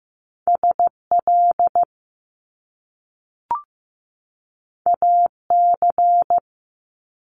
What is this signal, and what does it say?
Morse code, heard as a keyed tone, about 15 wpm, 700 Hz. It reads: SL AC
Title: Top Two Letter Combinations in Words 15wpm